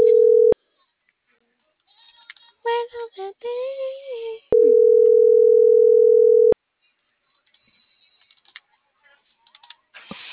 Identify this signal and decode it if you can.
Ringback tone (North America), 440 + 480 Hz.